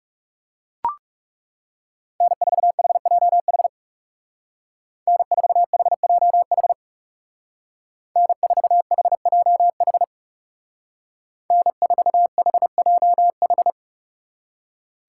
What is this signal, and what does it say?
Morse code, heard as a keyed tone, about 37 wpm, 700 Hz. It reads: N4HJH N4HJH N4HJH N4HJH
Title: Speed Racing - USA 1x3 Call Signs 30wpm